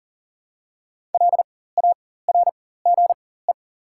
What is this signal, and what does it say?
Morse code, heard as a keyed tone, about 42 wpm, spaced as for 19 wpm, 700 Hz. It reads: LARGE
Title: 4x Character Spacing - Top 200 Words - 40wpm